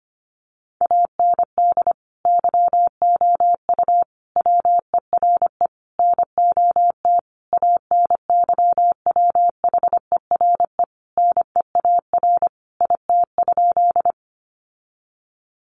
Morse code, heard as a keyed tone, about 25 wpm, 700 Hz. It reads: AND YOU WERE NOT ANYWHERE NEAR IT?
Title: Sentences from Top 1400 Words 25wpm